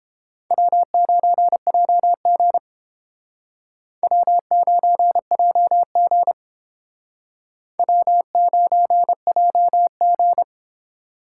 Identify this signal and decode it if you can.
Morse code, heard as a keyed tone, about 31 wpm, 700 Hz. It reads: W9JG W9JG W9JG